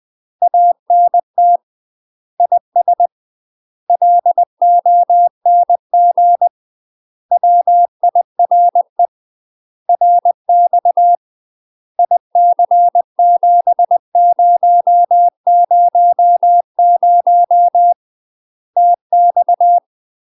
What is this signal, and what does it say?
Morse code, heard as a keyed tone, about 20 wpm, 700 Hz. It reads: ANT IS LONG WIRE RX IC7000 TX
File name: UK MASTER QSOs TRAINING / 20WPM / UK MASTER QSOs 0068